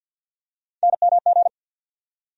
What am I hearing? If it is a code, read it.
Morse code, heard as a keyed tone, about 51 wpm, 690 Hz. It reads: NMG
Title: USA Call Sign Postfixes - 3 Letters 50wpm